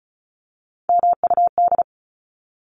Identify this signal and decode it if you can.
Morse code, heard as a keyed tone, about 34 wpm, 700 Hz. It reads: MUD
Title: Top 1751-2000 Words 35wpm